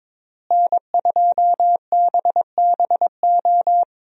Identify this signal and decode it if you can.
Morse code, heard as a keyed tone, about 22 wpm, 700 Hz. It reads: N2BBO
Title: USA Call Signs - 5 Characters - Repeated 2x - 22wpm